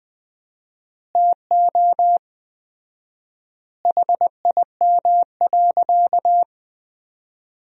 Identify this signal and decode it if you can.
Morse code, heard as a keyed tone, about 20 wpm, 700 Hz. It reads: TO HIM.
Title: 4x Word-Spacing - Sentences from Top 300 Words - 20wpm